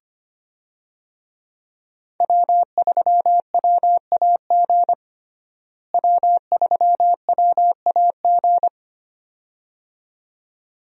Morse code, heard as a keyed tone, about 25 wpm, 700 Hz. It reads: W3WAG W3WAG
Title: USA Call Signs - 5 Characters - Repeated 2x - 25wpm